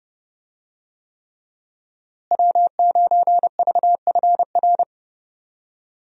Morse code, heard as a keyed tone, about 32 wpm, 700 Hz. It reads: W9VFR
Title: USA Call Signs - 5 Characters - Repeated 3x - 30wpm